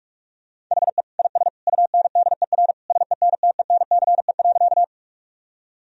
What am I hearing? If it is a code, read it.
Morse code, heard as a keyed tone, about 46 wpm, 700 Hz. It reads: HE IS UNDER SENTENCE.
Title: Sentences from Top 1000 Words - Review Part 1 - 45wpm